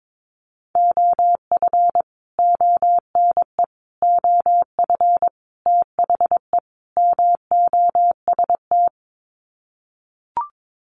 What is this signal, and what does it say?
Morse code, heard as a keyed tone, about 22 wpm, 700 Hz. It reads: OF ONE OF THE MOST
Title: Sets of 5 Words Encore2 22wpm